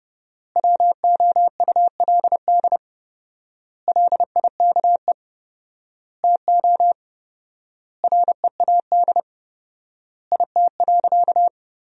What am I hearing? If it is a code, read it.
Morse code, heard as a keyed tone, about 32 wpm, 700 Hz. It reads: WOULD LIKE TO READ IT.
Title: 4x Word-Spacing - Sentences from Top 500 Words - 30wpm